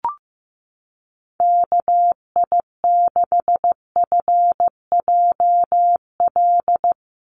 Morse code, heard as a keyed tone, about 15 wpm, 700 Hz. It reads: KI6FJL
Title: USA Call Signs 15wpm